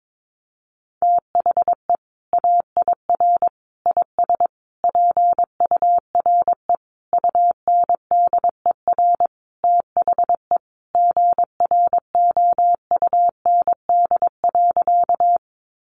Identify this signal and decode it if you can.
Morse code, heard as a keyed tone, about 22 wpm, 700 Hz. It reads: THE AIR IS PURE UNDER THE GROUND.
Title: Sentences from Top 1500 Words 22wpm